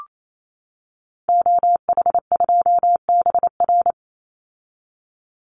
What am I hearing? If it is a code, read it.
Morse code, heard as a keyed tone, about 28 wpm, 700 Hz. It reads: OH2BR
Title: Call Signs - 28wpm